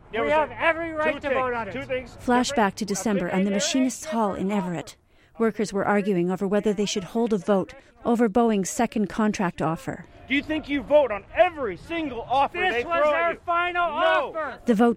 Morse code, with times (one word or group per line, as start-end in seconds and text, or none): none